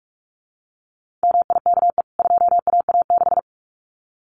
Morse code, heard as a keyed tone, about 44 wpm, 700 Hz. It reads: MIKE 2AA6
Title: CWT Exchange 45wpm